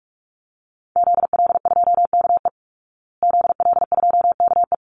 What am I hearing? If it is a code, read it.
Morse code, heard as a keyed tone, about 44 wpm, 700 Hz. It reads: ZL2KE ZL2KE
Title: Call Signs Repeated 3x - 45wpm